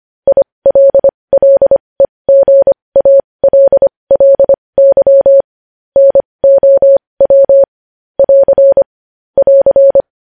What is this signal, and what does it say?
Morse code, heard as a keyed tone, about 25 wpm, spaced as for 20 wpm, 550 Hz. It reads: ILLEGALLY NOW <AR> <AR>
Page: news-headlines-in-morse-code-at-20-wpm